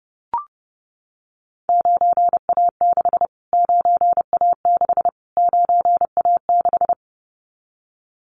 Morse code, heard as a keyed tone, about 30 wpm, 700 Hz. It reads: 9A6 9A6 9A6
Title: Callsign Prefixes for Top 25 DXCC - Repeated 3x - 30wpm